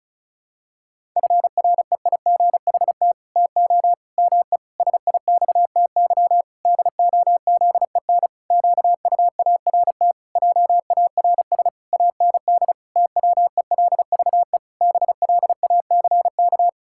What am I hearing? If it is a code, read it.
Morse code, heard as a keyed tone, about 36 wpm, 700 Hz. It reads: FREIGHT TO ME SIXTY DOZEN QUART JARS AND TWELVE BLACK
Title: Pangram Challenge v2 - 35wpm